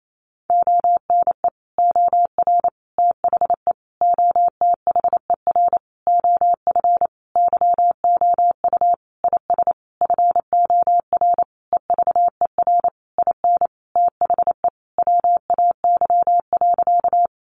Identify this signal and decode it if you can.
Morse code, heard as a keyed tone, about 28 wpm, 700 Hz. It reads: ONE OR THE OTHER OF YOU IS FOR EVER IN THE WAY.
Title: Sentences from Top 400 Words 28wpm